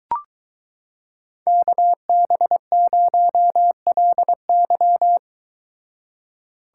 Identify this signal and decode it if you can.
Morse code, heard as a keyed tone, about 23 wpm, 700 Hz. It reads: KB0LY